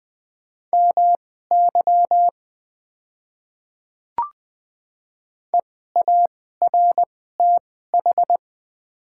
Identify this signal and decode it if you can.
Morse code, heard as a keyed tone, about 20 wpm, spaced as for 14 wpm, 700 Hz. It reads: MY EARTH
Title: 2x Character Spacing - Top 200 Words - 20wpm